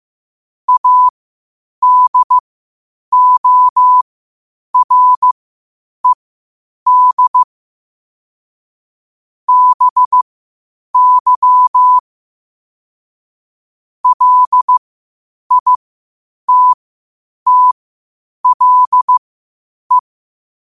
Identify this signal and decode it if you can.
Morse code, heard as a keyed tone, about 15 wpm, spaced as for 8 wpm, 1000 Hz. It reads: ADORED BY LITTLE
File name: qotdfiles / 8wpm